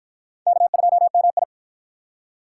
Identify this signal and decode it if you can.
Morse code, heard as a keyed tone, about 53 wpm, 700 Hz. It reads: K2MI